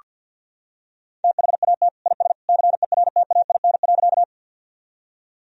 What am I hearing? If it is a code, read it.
Morse code, heard as a keyed tone, about 47 wpm, 700 Hz. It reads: THAT IS CERTAIN.
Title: Sentences from Top 1000 Words - Review Part 2 - 50wpm